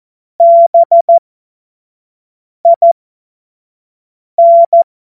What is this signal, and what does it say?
Morse code, heard as a keyed tone, about 14 wpm, spaced as for 5 wpm, 680 Hz. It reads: BIN
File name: Morsekurs / Witze / Witz 01. März 2023 25 F70